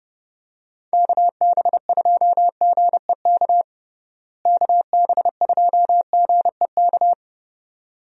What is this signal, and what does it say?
Morse code, heard as a keyed tone, about 32 wpm, 700 Hz. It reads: KB2GEK KB2GEK
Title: USA Call Signs - 6 Characters - Repeated 2x - 30wpm